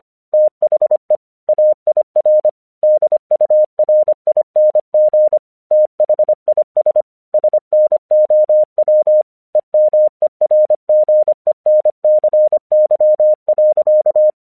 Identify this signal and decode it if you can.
Morse code, heard as a keyed tone, about 25 wpm, 610 Hz. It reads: THE AIR DURING THIS SNOW EMERGENCY.
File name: 1 English 25wpm / AQC08